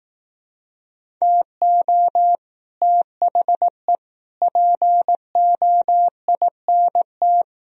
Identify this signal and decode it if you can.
Morse code, heard as a keyed tone, about 18 wpm, 700 Hz. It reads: TO THE POINT
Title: Speed-Racing - Sets of 4 Words 15wpm